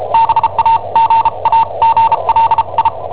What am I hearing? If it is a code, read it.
Morse code, heard as a keyed tone, about 34 wpm, spaced as for 28 wpm, 910 Hz. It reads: BAGAGLI
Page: Cassetta esercizi preparata da I7OHP per IW5ELP